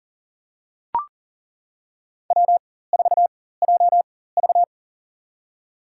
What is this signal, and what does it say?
Morse code, heard as a keyed tone, about 40 wpm, spaced as for 19 wpm, 700 Hz. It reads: W4JV
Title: USA Call Signs - 4 Characters - 4x Character Spacing - 40wpm